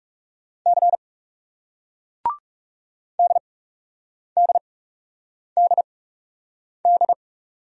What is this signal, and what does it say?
Morse code, heard as a keyed tone, about 41 wpm, spaced as for 8 wpm, 700 Hz. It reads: C DDDD